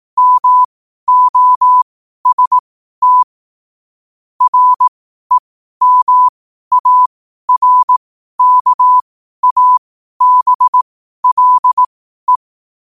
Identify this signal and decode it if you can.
Morse code, heard as a keyed tone, about 18 wpm, spaced as for 12 wpm, 1000 Hz. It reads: MOST REMARKABLE
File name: qotdfiles / 12wpm